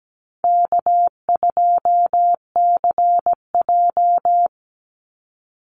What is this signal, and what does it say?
Morse code, heard as a keyed tone, about 17 wpm, 700 Hz. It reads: K2CJ
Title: Call Signs Repeated 2x - 17wpm